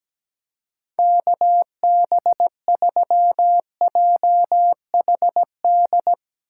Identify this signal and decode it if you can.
Morse code, heard as a keyed tone, about 17 wpm, 700 Hz. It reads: KB3JHD
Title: USA Call Signs - 6 Characters - 2x Character Spacing - 17wpm